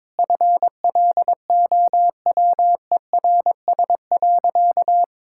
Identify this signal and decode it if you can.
Morse code, heard as a keyed tone, about 22 wpm, 700 Hz. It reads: FLOWERS.
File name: Sentences with 9-Word Phrases - 22wpm